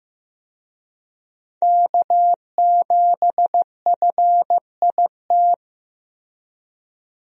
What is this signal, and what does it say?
Morse code, heard as a keyed tone, about 15 wpm, 700 Hz. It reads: K7FIT
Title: USA Call Signs - 5 Characters - Repeated 3x - 15wpm